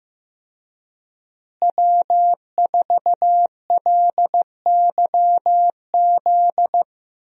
Morse code, heard as a keyed tone, about 15 wpm, 700 Hz. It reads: W4LYZ
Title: USA Call Signs - 5 Characters - Repeated 3x - 15wpm